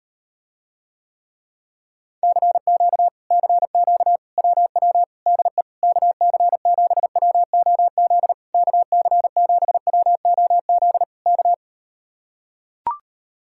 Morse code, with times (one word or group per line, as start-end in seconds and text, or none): 2.2-3.1 CQ
3.3-4.2 CQ
4.4-5.0 WW
5.3-5.6 DE
5.8-8.3 KC7WOZ
8.5-11.0 KC7WOZ
11.3-11.5 K